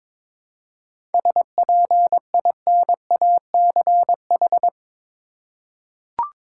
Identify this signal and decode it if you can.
Morse code, heard as a keyed tone, about 22 wpm, 690 Hz. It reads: SPINACH
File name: Difficult Words with Hints - 22wpm